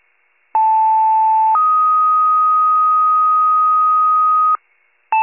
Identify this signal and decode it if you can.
Two-tone page, heard as a one-second tone, then a three-second tone, about 880 then 1250 Hz. It is heard from 0.5 to 4.6 s.